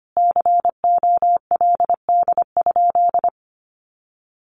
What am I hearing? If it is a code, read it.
Morse code, heard as a keyed tone, about 25 wpm, 700 Hz. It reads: COLD?
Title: Sentences from Top 400 Words 25wpm